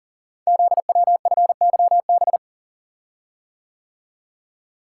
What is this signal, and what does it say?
Morse code, heard as a keyed tone, about 38 wpm, 700 Hz. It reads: ZWFYB
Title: Code Groups - 5 Letters - 40wpm